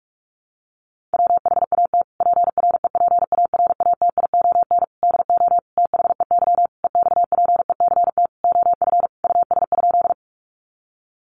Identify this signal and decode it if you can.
Morse code, heard as a keyed tone, about 44 wpm, 700 Hz. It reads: WHAT PREPARATION DO THEY EXPECT OF US?
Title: Sentences from Top 1400 Words 45wpm